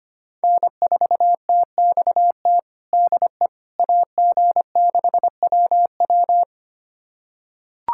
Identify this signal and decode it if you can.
Morse code, heard as a keyed tone, about 25 wpm, 700 Hz. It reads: N4TXT DE AG6WW
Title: USA Callsign DE Callsign - 5 Characters - 25wpm